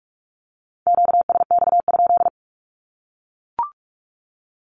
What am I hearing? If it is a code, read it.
Morse code, heard as a keyed tone, about 44 wpm, 700 Hz. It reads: QSX?
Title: Q-Codes 45wpm